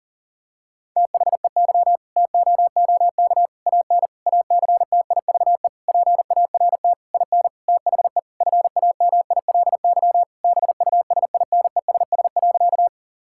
Morse code, heard as a keyed tone, about 40 wpm, 700 Hz. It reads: THEY TOOK AN ACTIVE PART IN THE FAMILY BUSINESS.
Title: Sentences from Top 1000 Words - Review Part 6 - 40wpm